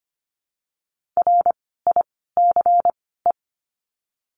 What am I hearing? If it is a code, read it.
Morse code, heard as a keyed tone, about 25 wpm, 700 Hz. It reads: R I C E
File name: course / 25:10wpm / 094 Lesson 16f - Letter c - Words 25wpm